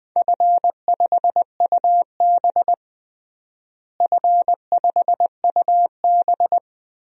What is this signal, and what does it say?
Morse code, heard as a keyed tone, about 20 wpm, 700 Hz. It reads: F5UB F5UB
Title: DXCC-100 Call Signs - 4 Characters - Repeated 3x - 20wpm